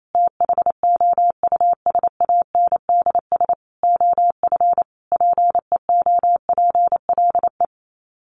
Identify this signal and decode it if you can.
Morse code, heard as a keyed tone, about 28 wpm, 700 Hz. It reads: THOUSANDS OF PEOPLE
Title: Sets of 5 Words Encore2 28wpm